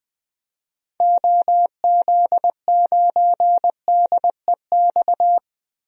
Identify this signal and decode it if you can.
Morse code, heard as a keyed tone, about 21 wpm, 700 Hz. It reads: OZ9DEX
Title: DXCC-100 Call Signs - 6 Characters - 20wpm